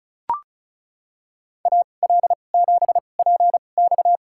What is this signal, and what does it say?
Morse code, heard as a keyed tone, about 34 wpm, spaced as for 25 wpm, 700 Hz. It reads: AL7PX
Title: USA Call Signs - 5 Characters - 2x Character Spacing - 35wpm